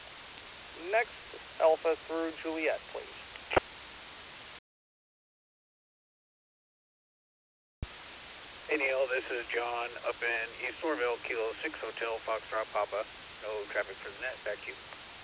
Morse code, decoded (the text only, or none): none